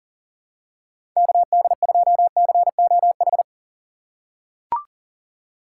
Morse code, heard as a keyed tone, about 40 wpm, 700 Hz. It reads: KD2COH
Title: USA Call Signs - 6 Characters - 3x Character Spacing - 40wpm